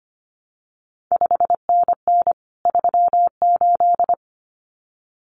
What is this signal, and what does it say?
Morse code, heard as a keyed tone, about 25 wpm, 700 Hz. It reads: 5NN 38
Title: CQ WW DX Contest Exchange Encore - 25wpm